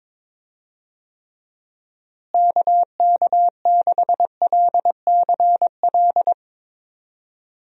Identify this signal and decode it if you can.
Morse code, heard as a keyed tone, about 22 wpm, 700 Hz. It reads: KK6LCL
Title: USA Call Signs - 6 Characters - Repeated 2x - 22wpm